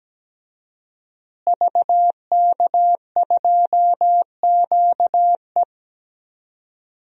Morse code, heard as a keyed tone, about 17 wpm, 700 Hz. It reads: VK2QE
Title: DXCC-100 Call Signs - 5 Characters - Repeated 3x - 17wpm